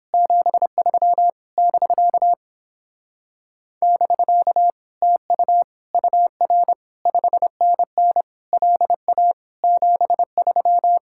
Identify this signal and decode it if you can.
Morse code, heard as a keyed tone, about 29 wpm, 700 Hz. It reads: 73 <BK> <BK> TU UR 5NN LA 73